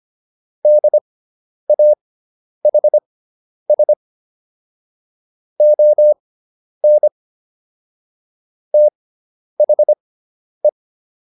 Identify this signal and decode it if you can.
Morse code, heard as a keyed tone, about 25 wpm, spaced as for 10 wpm, 600 Hz. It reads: DAHS ON THE